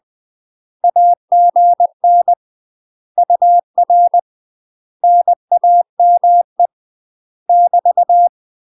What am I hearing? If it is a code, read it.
Morse code, heard as a keyed tone, about 20 wpm, 700 Hz. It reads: AGN UR NAME <BT>
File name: UK MASTER QSOs TRAINING / 20WPM / UK MASTER QSOs 0020